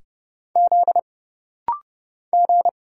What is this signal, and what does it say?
Morse code, heard as a keyed tone, about 30 wpm, 700 Hz. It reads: Z G
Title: Single Letter-Number Alt - Rapid-Fire 30wpm